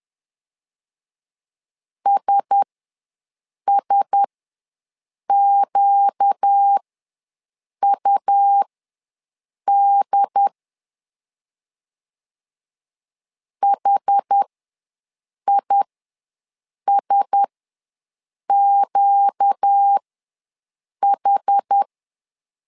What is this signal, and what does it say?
Morse code, heard as a keyed tone, about 10 wpm, spaced as for 6 wpm, 800 Hz. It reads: SSQUD HISQH
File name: LEZIONI / 4 Q1S ws 10wpm 2208